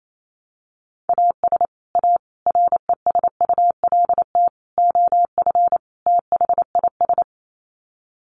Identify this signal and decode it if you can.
Morse code, heard as a keyed tone, about 28 wpm, 700 Hz. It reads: AS A RESULT OF THIS
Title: Sets of 5 Words 28wpm